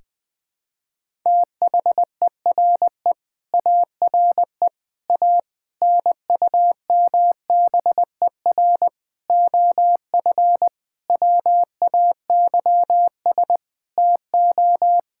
Morse code, heard as a keyed tone, about 21 wpm, 700 Hz. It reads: THERE ARE A NUMBER OF WAYS TO
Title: Sentences with 7-Word Phrases - 20wpm